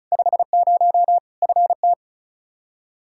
Morse code, heard as a keyed tone, about 36 wpm, 700 Hz. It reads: 50 FT
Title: Practice QSO - Exchange 2 - Info - 35wpm